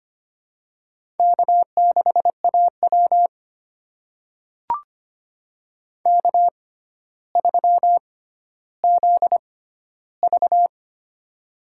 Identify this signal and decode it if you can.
Morse code, heard as a keyed tone, about 25 wpm, spaced as for 13 wpm, 700 Hz. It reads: K6AW K3ZV
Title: USA Call Signs - 4 Characters - 6x Character Spacing - 25wpm